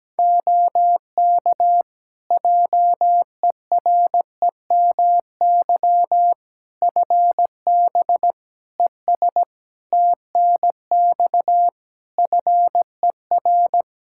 Morse code, heard as a keyed tone, about 17 wpm, 700 Hz. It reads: OK JEREMY FB ES TNX FER
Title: Practice QSO - Exchange 2 - Info - 17wpm